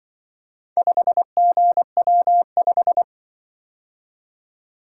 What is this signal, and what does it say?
Morse code, heard as a keyed tone, about 24 wpm, 700 Hz. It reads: 5GW5
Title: Speed-Racing - Four Letter-Number Combinations - 20wpm